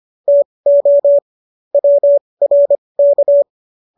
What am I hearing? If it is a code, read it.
Morse code, heard as a keyed tone, about 25 wpm, spaced as for 20 wpm, 570 Hz. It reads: TO WRK